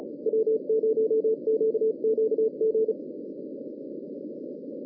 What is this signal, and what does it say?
Morse code, heard as a keyed tone, about 34 wpm, 440 Hz. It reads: W0QQG